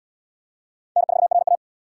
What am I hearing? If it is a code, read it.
Morse code, heard as a keyed tone, about 77 wpm, 700 Hz. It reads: N4RA